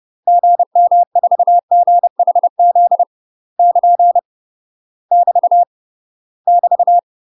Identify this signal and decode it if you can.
Morse code, heard as a keyed tone, about 30 wpm, 700 Hz. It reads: GM4GHZ <KN> <BT> <BT>
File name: UK MASTER QSOs TRAINING / 30WPM / UK MASTER QSOs 0028